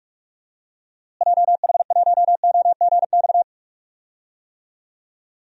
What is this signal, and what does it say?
Morse code, heard as a keyed tone, about 46 wpm, 700 Hz. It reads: JH1OGX
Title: DXCC-100 Call Signs - 6 Characters - Repeated 3x - 45wpm